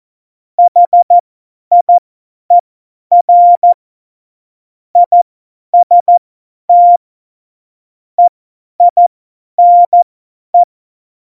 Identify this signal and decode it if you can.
Morse code, heard as a keyed tone, about 14 wpm, spaced as for 10 wpm, 700 Hz. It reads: HIER IST EINE
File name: Morsekurs / Witze / Witz 09. April 2019, 50 F70